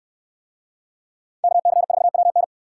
Morse code, heard as a keyed tone, about 65 wpm, 700 Hz. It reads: KC4WN